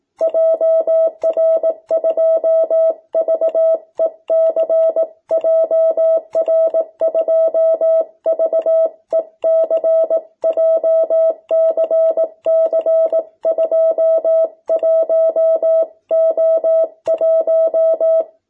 Morse code, heard as a keyed tone, about 18 wpm, 660 Hz. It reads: JR2VEC JR2VEC JCC21O1
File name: jh8cbh / 2021cw